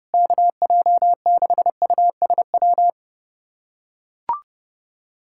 Morse code, heard as a keyed tone, about 32 wpm, 700 Hz. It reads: KJ6USW
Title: USA Callsign DE Callsign - 6 Characters - 30wpm